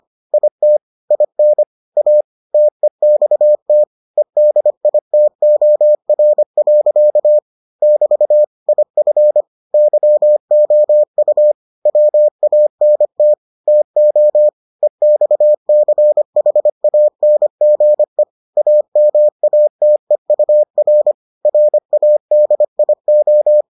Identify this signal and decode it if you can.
Morse code, heard as a keyed tone, about 25 wpm, 600 Hz. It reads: IT IN A TEXT EDITOR. <BT> IF YOU WANT TO EXCHANGE AMATEUR RADIO